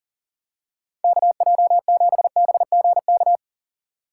Morse code, heard as a keyed tone, about 40 wpm, 700 Hz. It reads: KJ7BGK
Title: USA Call Signs - 6 Characters - 4x Character Spacing - 40wpm